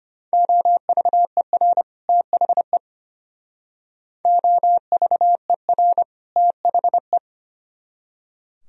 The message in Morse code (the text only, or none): OVER THE OVER THE